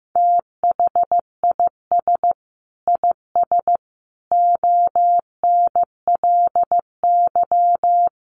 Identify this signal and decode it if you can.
Morse code, heard as a keyed tone, about 14 wpm, 700 Hz. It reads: THIS IS ONLY
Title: Sentences from Top 1400 Words 15wpm